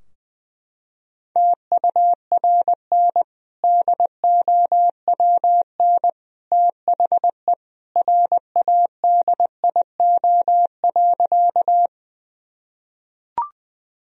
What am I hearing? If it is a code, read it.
Morse code, heard as a keyed tone, about 21 wpm, 700 Hz. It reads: TURN DOWN THE RADIO.
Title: Sentences from Top 1000 Words - Review Part 3 - 20wpm